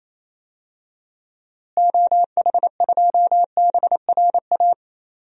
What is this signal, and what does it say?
Morse code, heard as a keyed tone, about 28 wpm, 700 Hz. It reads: OH2BRA
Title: DXCC-100 Call Signs - 6 Characters - Repeated 2x - 28wpm